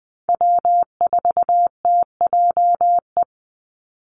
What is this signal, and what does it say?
Morse code, heard as a keyed tone, about 20 wpm, 700 Hz. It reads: W4TJE
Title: Call Signs Repeated 2x - 20wpm